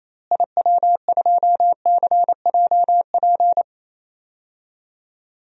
Morse code, heard as a keyed tone, about 28 wpm, 700 Hz. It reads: IW2CJP
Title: DXCC-100 Call Signs - 6 Characters - Repeated 3x - 28wpm